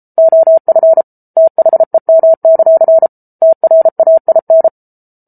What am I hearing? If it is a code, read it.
Morse code, heard as a keyed tone, about 34 wpm, 650 Hz. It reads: OF THEM; TRAIN